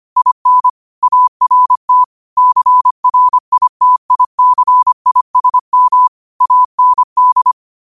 Morse code, heard as a keyed tone, about 25 wpm, 1000 Hz. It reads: IN ART CRITICISM AND
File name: qotdfiles / 25wpm